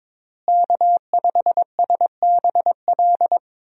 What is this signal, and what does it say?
Morse code, heard as a keyed tone, about 22 wpm, 700 Hz. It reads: K5SBL
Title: USA Call Signs - 5 Characters - Repeated 2x - 22wpm